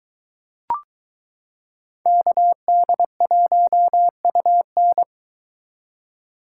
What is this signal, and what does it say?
Morse code, heard as a keyed tone, about 22 wpm, 700 Hz. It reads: KD1UN